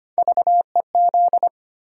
Morse code, heard as a keyed tone, about 25 wpm, 700 Hz. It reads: VEZ
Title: USA Call Sign Postfixes - 3 Letters 25wpm